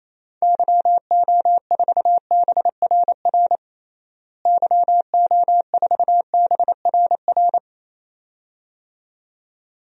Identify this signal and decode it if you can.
Morse code, heard as a keyed tone, about 28 wpm, 700 Hz. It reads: YO4BRR YO4BRR
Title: DXCC-100 Call Signs - 6 Characters - Repeated 2x - 28wpm